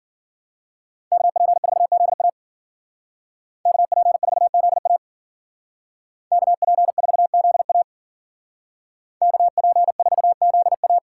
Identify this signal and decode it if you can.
Morse code, heard as a keyed tone, about 52 wpm, 700 Hz. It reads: KP4ZA KP4ZA KP4ZA KP4ZA